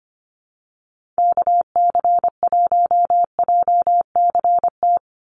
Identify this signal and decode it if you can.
Morse code, heard as a keyed tone, about 25 wpm, 700 Hz. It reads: KC1JCT
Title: USA Call Signs 25wpm